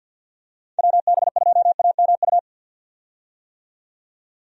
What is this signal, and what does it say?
Morse code, heard as a keyed tone, about 50 wpm, 700 Hz. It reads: WB2AMU
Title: USA Call Signs - 6 Characters - 50wpm